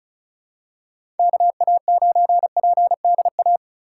Morse code, heard as a keyed tone, about 34 wpm, 700 Hz. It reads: KA9PDA
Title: USA Call Signs - 2x Character Spacing - 35wpm